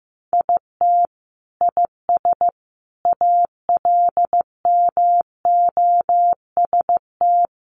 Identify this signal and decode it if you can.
Morse code, heard as a keyed tone, about 15 wpm, 700 Hz. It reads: IT IS ALMOST
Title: Sentences from Top 1200 Words 15wpm